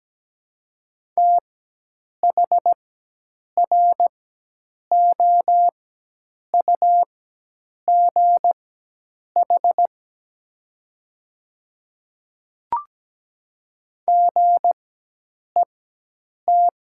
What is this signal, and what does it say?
Morse code, heard as a keyed tone, about 17 wpm, spaced as for 8 wpm, 700 Hz. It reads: THROUGH GET
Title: 4x Character Spacing - Top 300 Words - 17wpm